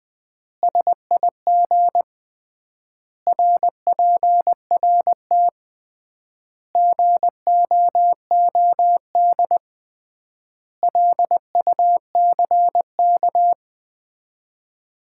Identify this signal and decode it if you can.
Morse code, heard as a keyed tone, about 21 wpm, 700 Hz. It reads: SIG RPRT GOOD LUCK
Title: Common QSO Phrases - 3x Word-Spacing - 20wpm